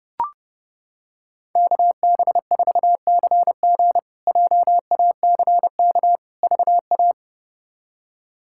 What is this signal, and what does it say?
Morse code, heard as a keyed tone, about 32 wpm, 700 Hz. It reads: KB4CG JACK VA
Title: K1USN SST QSO Practice - 30wpm